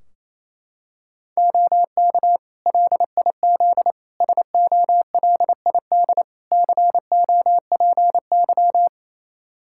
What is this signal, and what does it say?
Morse code, heard as a keyed tone, about 28 wpm, 700 Hz. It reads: OK LIZ SOLID COPY
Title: K1USN SST QSO Practice 28wpm